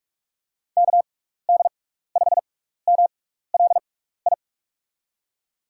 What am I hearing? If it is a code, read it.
Morse code, heard as a keyed tone, about 44 wpm, spaced as for 15 wpm, 700 Hz. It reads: KD5MLI
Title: USA Call Signs - 6 Characters - 6x Character Spacing - 45wpm